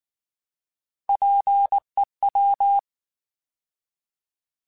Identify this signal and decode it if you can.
Morse code, heard as a keyed tone, about 19 wpm, 800 Hz. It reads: PEW